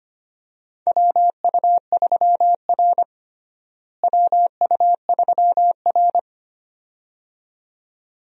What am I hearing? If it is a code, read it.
Morse code, heard as a keyed tone, about 25 wpm, 700 Hz. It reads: WU3R WU3R
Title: USA Call Signs - 4 Characters - Repeated 2x - 25wpm